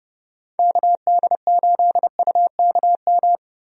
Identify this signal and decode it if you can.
Morse code, heard as a keyed tone, about 30 wpm, 700 Hz. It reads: KD8UKM
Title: USA Call Signs - 6 Characters - Repeated 2x - 30wpm